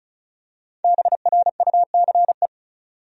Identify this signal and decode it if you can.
Morse code, heard as a keyed tone, about 36 wpm, 700 Hz. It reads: BRUCE
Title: USA Common Names - Max 5 Letters - 35wpm